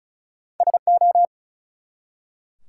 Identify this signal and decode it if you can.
Morse code, heard as a keyed tone, about 34 wpm, 700 Hz. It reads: SO
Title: Two Letter Words - 35wpm